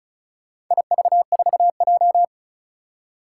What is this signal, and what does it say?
Morse code, heard as a keyed tone, about 36 wpm, 690 Hz. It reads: IV4J
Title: Four Challenging Letter-Number Combinations 35wpm